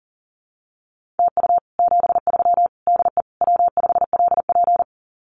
Morse code, heard as a keyed tone, about 40 wpm, 700 Hz. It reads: TU 73 DE W5LP
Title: Common QSO Phrases Encore 40wpm